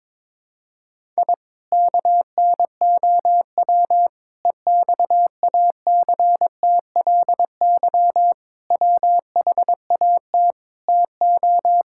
Morse code, heard as a keyed tone, about 22 wpm, 700 Hz. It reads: I KNOW EXACTLY WHAT TO
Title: Sentences from Top 1000 Words - Review Part 6 - 22wpm